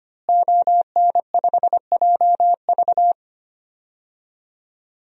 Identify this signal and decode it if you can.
Morse code, heard as a keyed tone, about 25 wpm, 700 Hz. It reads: ON5JV
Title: DXCC-100 Call Signs - 5 Characters - Repeated 2x - 25wpm